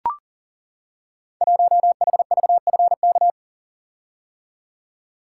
Morse code, heard as a keyed tone, about 42 wpm, 700 Hz. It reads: 1HVFK
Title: Code Groups - 5 Letter:Numbers - 40wpm